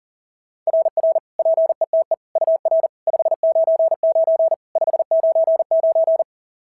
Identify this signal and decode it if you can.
Morse code, heard as a keyed tone, about 40 wpm, 690 Hz. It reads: RR PETE UR 599 599
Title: SOTA QSO Practice 40wpm